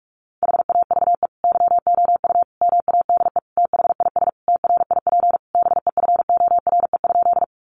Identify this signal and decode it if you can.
Morse code, heard as a keyed tone, about 46 wpm, 700 Hz. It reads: HAVE YOU MADE THIS TRIP BEFORE?
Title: Sentences from Top 1300 Words 45wpm